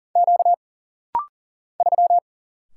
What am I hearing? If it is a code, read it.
Morse code, heard as a keyed tone, about 40 wpm, 700 Hz. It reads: Q 3
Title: Single Letter-Number - Rapid-Fire 40wpm